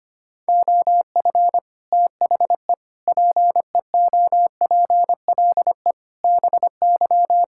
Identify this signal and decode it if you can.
Morse code, heard as a keyed tone, about 25 wpm, 700 Hz. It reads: OF THE PEOPLE BY
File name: Sets of 9 Words - 25wpm